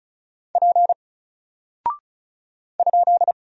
Morse code, heard as a keyed tone, about 36 wpm, 700 Hz. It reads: P ?
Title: Single Character (Letter:Number:Punctuation:Prosign) 35wpm